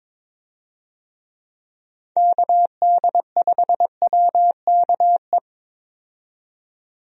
Morse code, heard as a keyed tone, about 22 wpm, 700 Hz. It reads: KD5WKE
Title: USA Call Signs - 6 Characters - 22wpm